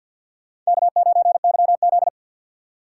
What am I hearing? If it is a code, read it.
Morse code, heard as a keyed tone, about 50 wpm, 700 Hz. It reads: K9YZ
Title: USA Call Signs - 4 Characters - 2x Character Spacing - 50wpm